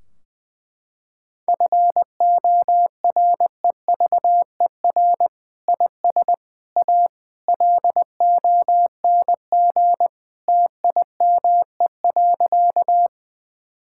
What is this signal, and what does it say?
Morse code, heard as a keyed tone, about 20 wpm, 700 Hz. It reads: FOREVER IS A LONG TIME.
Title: Sentences from Top 1600 Words 20wpm